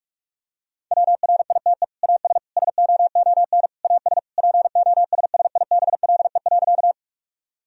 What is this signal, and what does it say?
Morse code, heard as a keyed tone, about 47 wpm, 700 Hz. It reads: WRITE AS SOON AS POSSIBLE.
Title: Sentences from Top 1000 Words - Review Part 1 - 45wpm